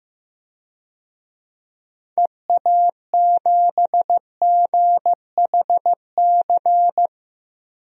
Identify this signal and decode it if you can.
Morse code, heard as a keyed tone, about 15 wpm, 700 Hz. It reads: EA7GHC
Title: DXCC-100 Call Signs - 6 Characters - 15wpm